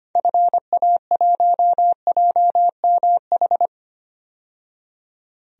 Morse code, heard as a keyed tone, about 25 wpm, 700 Hz. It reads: FA1JMH
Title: DXCC-100 Call Signs - 6 Characters - 25wpm